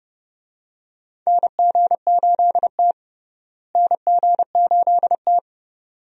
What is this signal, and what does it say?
Morse code, heard as a keyed tone, about 32 wpm, 700 Hz. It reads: NG8T NG8T
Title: USA Call Signs - 4 Characters - Repeated 2x - 30wpm